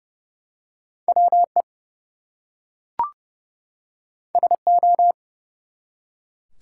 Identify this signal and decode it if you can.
Morse code, heard as a keyed tone, about 29 wpm, 700 Hz. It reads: WE SO
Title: Two Letter Words - 30wpm